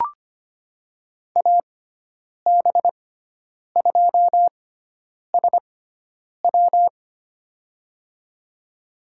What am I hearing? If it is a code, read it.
Morse code, heard as a keyed tone, about 25 wpm, spaced as for 9 wpm, 700 Hz. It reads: AB2SW
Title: USA Call Signs - 6x Character Spacing - 25wpm